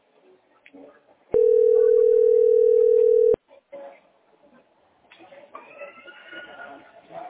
Ringback tone (North America), 440 + 480 Hz.